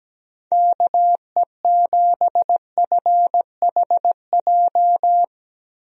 Morse code, heard as a keyed tone, about 17 wpm, 700 Hz. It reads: KE7FHJ
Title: USA Call Signs - 6 Characters - Repeated 2x - 17wpm